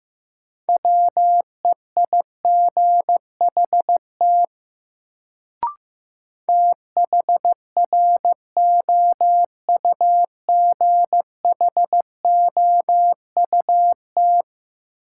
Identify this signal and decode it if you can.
Morse code, heard as a keyed tone, about 15 wpm, 700 Hz. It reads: WEIGHT THROUGHOUT
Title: Sentences from Top 800 Words - New Word Drill - 15wpm